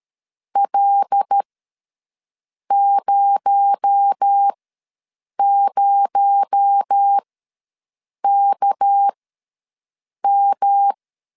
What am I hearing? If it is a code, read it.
Morse code, heard as a keyed tone, about 13 wpm, spaced as for 6 wpm, 800 Hz. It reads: L00KM
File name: LEZIONI / 6 PKX0 / 6 PKX0 ws 12wpm 3448